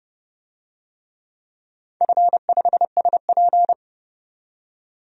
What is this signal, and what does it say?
Morse code, heard as a keyed tone, about 30 wpm, 700 Hz. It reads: F5SP
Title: DXCC-100 Call Signs - 4 Characters - 30wpm